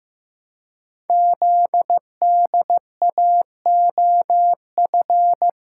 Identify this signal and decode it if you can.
Morse code, heard as a keyed tone, about 15 wpm, 700 Hz. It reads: ZDAOF
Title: Code Groups - 5 Letter:Numbers - 15wpm